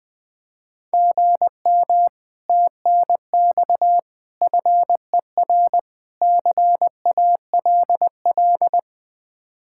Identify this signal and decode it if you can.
Morse code, heard as a keyed tone, about 20 wpm, 700 Hz. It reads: GM TNX FER CALL
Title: Speed-Racing Common QSO Phrases Encore - 20wpm